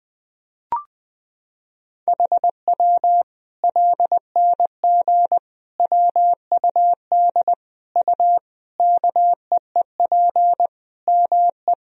Morse code, heard as a keyed tone, about 20 wpm, 700 Hz. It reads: HW LNG WUD U KEEP ME
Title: Sentences from Top 500 Words and Abbreviations 20wpm